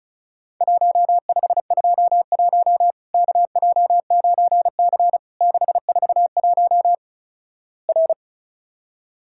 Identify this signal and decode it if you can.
Morse code, heard as a keyed tone, about 36 wpm, 700 Hz. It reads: 1521 KJ9C 641 R
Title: Worked All Europe DX Contest - QTC Exchange - 35wpm